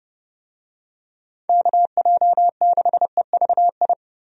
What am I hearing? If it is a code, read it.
Morse code, heard as a keyed tone, about 30 wpm, 700 Hz. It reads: KJ6EVI
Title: USA Call Signs - 6 Characters - 30wpm